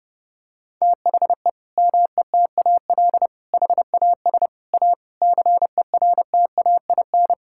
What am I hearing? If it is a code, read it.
Morse code, heard as a keyed tone, about 30 wpm, 700 Hz. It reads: THE METAL HAS A CERTAIN
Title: Sentences from Top 1000 Words - Review Part 6 - 30wpm